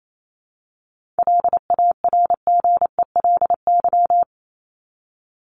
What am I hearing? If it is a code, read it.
Morse code, heard as a keyed tone, about 28 wpm, 700 Hz. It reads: LARGELY
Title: LY Suffix - Top 100 Words with Base Words 28wpm